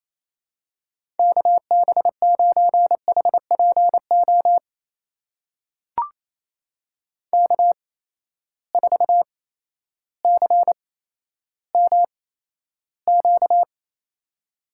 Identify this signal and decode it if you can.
Morse code, heard as a keyed tone, about 28 wpm, spaced as for 13 wpm, 700 Hz. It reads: KB9HPO K4CMQ
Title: USA Call Signs - 8x Character Spacing - 28wpm